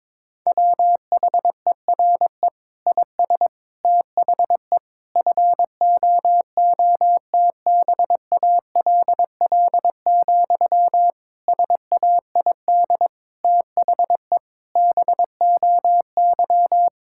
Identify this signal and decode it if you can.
Morse code, heard as a keyed tone, about 22 wpm, 700 Hz. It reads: WHERE IS THE FOOTBALL, SAID THE BOY
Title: Sentences from Top 1600 Words 22wpm